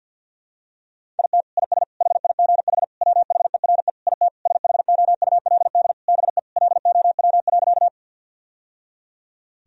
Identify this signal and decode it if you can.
Morse code, heard as a keyed tone, about 48 wpm, 700 Hz. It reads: IT IS HIGH WHERE IT SHOULD BE LOW.
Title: Sentences from Top 1000 Words - Review Part 1 - 50wpm